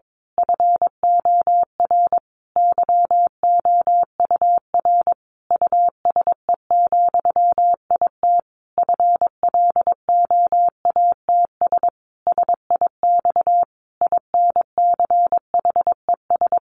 Morse code, heard as a keyed tone, about 22 wpm, 700 Hz. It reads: FOR YOUR USE, IT FLOATS SIX INCHES
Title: A Sound Of Thunder 22wpm